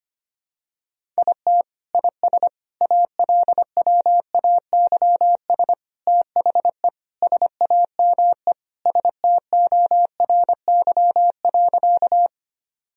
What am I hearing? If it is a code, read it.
Morse code, heard as a keyed tone, about 25 wpm, 700 Hz. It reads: IT IS ALWAYS THE SAME STORY.
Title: Sentences from Top 1000 Words - Review Part 1 - 25wpm